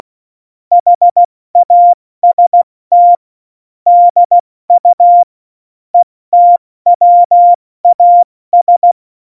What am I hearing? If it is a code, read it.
Morse code, heard as a keyed tone, about 16 wpm, 700 Hz. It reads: HAST DU ETWAS
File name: Morsekurs / Witze / Witz 08. Dezember 2020, 70 F80